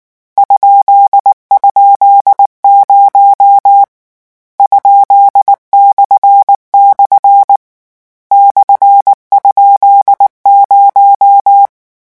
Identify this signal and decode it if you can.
Morse code, heard as a keyed tone, about 19 wpm, 800 Hz. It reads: ??0 ?// /?0
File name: morse mp3 / lecon13-vitesse 18